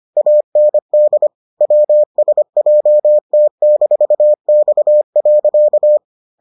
Morse code, heard as a keyed tone, about 25 wpm, 600 Hz. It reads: AND WSJT-X.